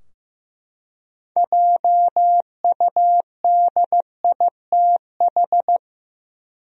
Morse code, heard as a keyed tone, about 15 wpm, 700 Hz. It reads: JUDITH
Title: USA Common Names - 15wpm